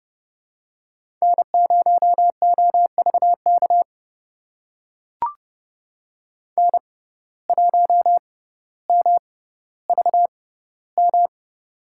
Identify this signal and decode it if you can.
Morse code, heard as a keyed tone, about 30 wpm, spaced as for 15 wpm, 700 Hz. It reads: N0OVK N1MVM
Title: USA Call Signs - 5 Characters - 6x Character Spacing - 30wpm